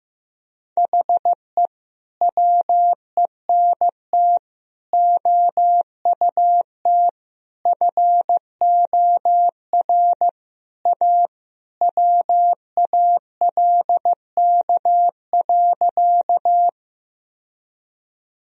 Morse code, heard as a keyed tone, about 14 wpm, 700 Hz. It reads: HE WENT OUT FOR A WALK.
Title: Sentences from Top 1000 Words - Review Part 1 - 15wpm